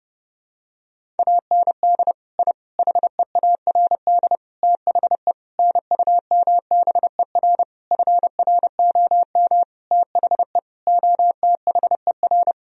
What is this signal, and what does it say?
Morse code, heard as a keyed tone, about 30 wpm, 700 Hz. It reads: AND I HEARD THE NUMBER FROM THE OTHER
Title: Sentences from Top 1000 Words - Review Part 1 - 30wpm